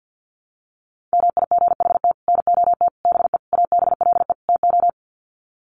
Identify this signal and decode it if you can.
Morse code, heard as a keyed tone, about 50 wpm, 700 Hz. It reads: MIGHT NOT BE ABLE TO
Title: Sets of 5 Words Encore2 50wpm